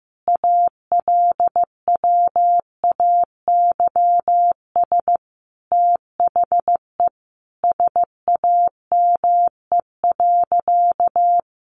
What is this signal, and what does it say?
Morse code, heard as a keyed tone, about 15 wpm, 700 Hz. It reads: ALWAYS THE SAME.
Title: Sentences from Top 300 Words 15wpm